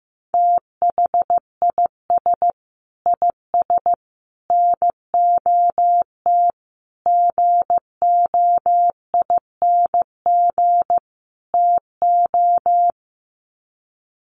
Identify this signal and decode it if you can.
Morse code, heard as a keyed tone, about 15 wpm, 700 Hz. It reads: THIS IS NOT GOING TO
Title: Sets of 5 Words Encore2 15wpm